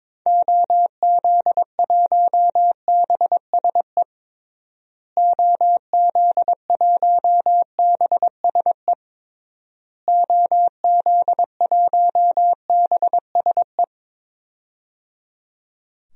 Morse code, heard as a keyed tone, about 22 wpm, 700 Hz. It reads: OZ1BSE OZ1BSE OZ1BSE
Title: DXCC-100 Call Signs - 6 Characters - Repeated 3x - 22wpm